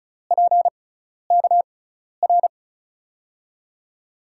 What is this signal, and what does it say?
Morse code, heard as a keyed tone, about 34 wpm, spaced as for 12 wpm, 700 Hz. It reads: PKR
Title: USA Call Signs - 6 Characters - 6x Character Spacing - 35wpm